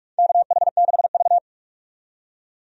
Morse code, heard as a keyed tone, about 46 wpm, 700 Hz. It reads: KH6V
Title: USA Call Signs - 4 Characters - 45wpm